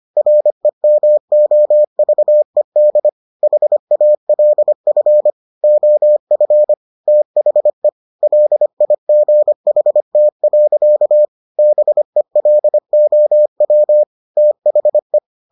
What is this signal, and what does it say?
Morse code, heard as a keyed tone, about 25 wpm, 600 Hz. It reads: REMOVED HALF OF THE LIGHT. BELOW THE